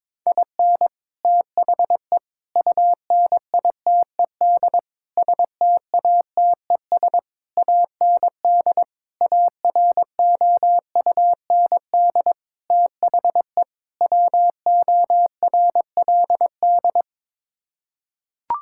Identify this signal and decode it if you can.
Morse code, heard as a keyed tone, about 22 wpm, 700 Hz. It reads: IN THE UNITED STATES AND AROUND THE WORLD
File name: practice / 22wpm / Sets of 8 Words - 22wpm